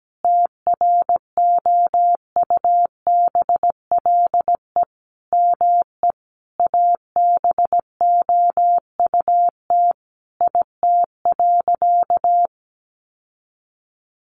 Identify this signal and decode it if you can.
Morse code, heard as a keyed tone, about 17 wpm, 700 Hz. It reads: TROUBLE ME ABOUT IT.
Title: Sentences from Top 1200 Words 17wpm